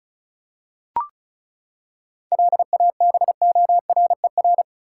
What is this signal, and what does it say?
Morse code, heard as a keyed tone, about 34 wpm, 700 Hz. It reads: LABORER
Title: K1USN SST QSO Practice 35wpm